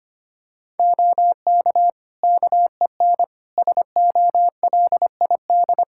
Morse code, heard as a keyed tone, about 25 wpm, 700 Hz. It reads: OK KEN SOLID
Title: Practice QSO - Exchange 3 - More Info - 25wpm